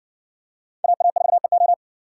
Morse code, heard as a keyed tone, about 57 wpm, 700 Hz. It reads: AA4EO